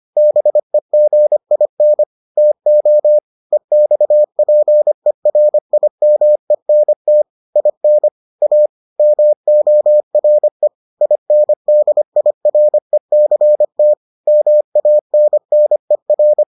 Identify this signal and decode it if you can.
Morse code, heard as a keyed tone, about 25 wpm, 600 Hz. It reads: BEGIN TO EXPERIMENT IN A MORE INDIRECT MANNER